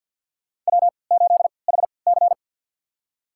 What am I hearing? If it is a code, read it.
Morse code, heard as a keyed tone, about 52 wpm, spaced as for 28 wpm, 700 Hz. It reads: W8HC
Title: USA Call Signs - 4 Characters - 3x Character Spacing - 50wpm